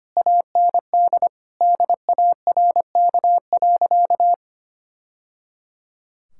Morse code, heard as a keyed tone, about 25 wpm, 700 Hz. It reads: AND DARK.
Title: Sentences from Top 1000 Words - Review Part 5 - 25wpm